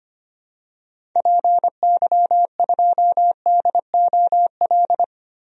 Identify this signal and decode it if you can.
Morse code, heard as a keyed tone, about 25 wpm, 700 Hz. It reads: PY2DOL
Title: DXCC-100 Call Signs - 6 Characters - Repeated 2x - 25wpm